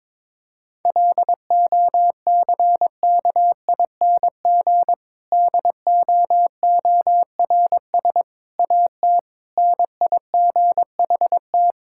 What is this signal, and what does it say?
Morse code, heard as a keyed tone, about 22 wpm, 700 Hz. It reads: LOCKING DOORS AT NIGHT
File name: Difficult Sentences with Hints - 22wpm